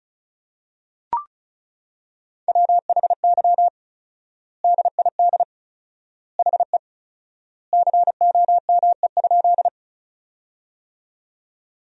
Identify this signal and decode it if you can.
Morse code, heard as a keyed tone, about 34 wpm, 700 Hz. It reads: WHY DID HE COME?
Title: 4x Word-Spacing - Sentences from Top 200 Words - 35wpm